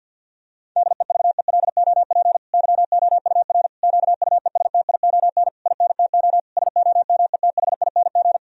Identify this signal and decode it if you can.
Morse code, heard as a keyed tone, about 50 wpm, 700 Hz. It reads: DEVELOP YOUR QUESTION INTO SOMETHING
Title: Sentences from Top 1000 Words - Review Part 2 - 50wpm